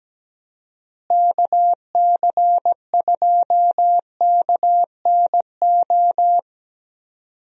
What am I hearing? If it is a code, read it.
Morse code, heard as a keyed tone, about 17 wpm, 700 Hz. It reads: KC2KNO
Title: USA Call Signs - 6 Characters - Repeated 2x - 17wpm